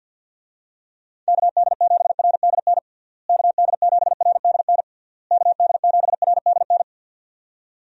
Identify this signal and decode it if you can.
Morse code, heard as a keyed tone, about 45 wpm, 700 Hz. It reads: KD7RDN KD7RDN KD7RDN
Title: USA Call Signs - 6 Characters - Repeated 3x - 50wpm